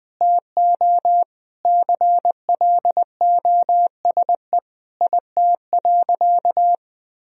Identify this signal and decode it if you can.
Morse code, heard as a keyed tone, about 19 wpm, 700 Hz. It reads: TO CLOSE IT.
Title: Sentences from Top 1000 Words - Review Part 1 - 20wpm